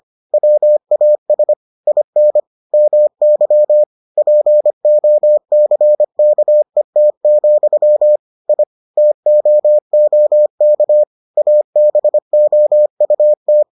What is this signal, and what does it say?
Morse code, heard as a keyed tone, about 25 wpm, 600 Hz. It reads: WAS IN MY POCKET, I TOOK ABOUT